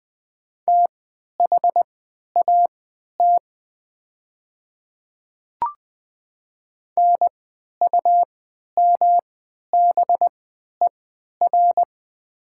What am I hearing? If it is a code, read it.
Morse code, heard as a keyed tone, about 20 wpm, spaced as for 11 wpm, 700 Hz. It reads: THAT NUMBER
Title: 3x Character Spacing - Top 100 Words - 20wpm